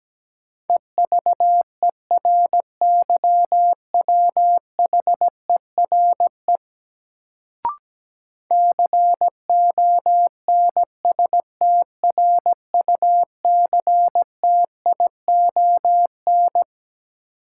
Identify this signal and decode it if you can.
Morse code, heard as a keyed tone, about 17 wpm, 700 Hz. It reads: EVERYWHERE CONSTRUCTION
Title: Sentences from Top 1600 Words - New Word Drill Alt - 17wpm